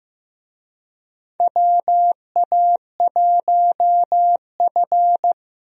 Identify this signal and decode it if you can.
Morse code, heard as a keyed tone, about 15 wpm, 700 Hz. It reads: WA1F
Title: USA Call Signs - 4 Characters - Repeated 2x - 15wpm